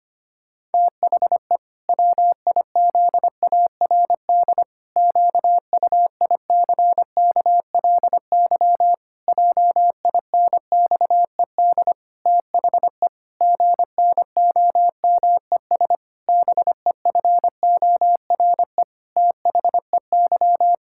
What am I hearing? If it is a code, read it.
Morse code, heard as a keyed tone, about 25 wpm, 700 Hz. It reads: THE WIZARD QUICKLY JINXED THE GNOMES BEFORE THEY
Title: Pangram Challenge v2 - 25wpm